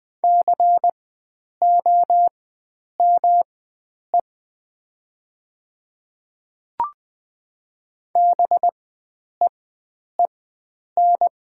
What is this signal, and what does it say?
Morse code, heard as a keyed tone, about 20 wpm, spaced as for 9 wpm, 700 Hz. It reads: COME BEEN
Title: 4x Character Spacing - Top 100 Words - 20wpm